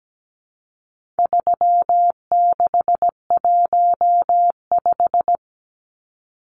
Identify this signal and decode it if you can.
Morse code, heard as a keyed tone, about 17 wpm, 700 Hz. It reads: 3615
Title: Numbers - 4-Digit 17wpm